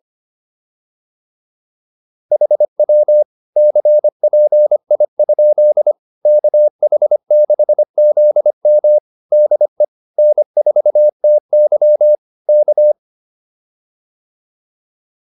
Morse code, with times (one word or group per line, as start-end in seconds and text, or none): 2.3-3.2 HW
3.6-5.9 CPI?
6.3-9.0 KH6ZM
9.3-9.8 DE
10.2-12.2 N4TY
12.5-12.9 K